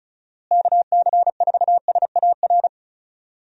Morse code, heard as a keyed tone, about 34 wpm, 700 Hz. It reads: KC4SAR
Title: USA Call Signs - 6 Characters - Repeated 2x - 35wpm